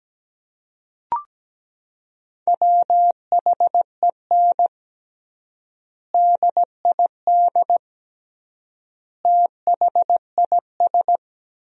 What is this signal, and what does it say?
Morse code, heard as a keyed tone, about 17 wpm, 700 Hz. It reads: WHEN DID THIS
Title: 3x Word-Spacing - Sentences from Top 400 Words - 17wpm